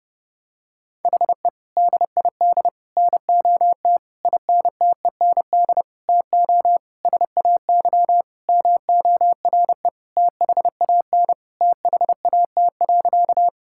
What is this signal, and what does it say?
Morse code, heard as a keyed tone, about 30 wpm, 700 Hz. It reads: HE DID NOT INTEND TO SAY MORE THAN THAT.
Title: Sentences from Top 1600 Words 30wpm